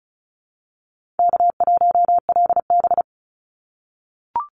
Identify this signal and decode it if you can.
Morse code, heard as a keyed tone, about 34 wpm, 700 Hz. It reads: K1LB
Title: NA SOTA Chaser Callsigns 35wpm